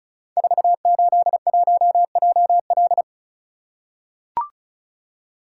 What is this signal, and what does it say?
Morse code, heard as a keyed tone, about 36 wpm, 690 Hz. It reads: 481JL
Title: Code Groups - 5 Letter:Numbers - 35wpm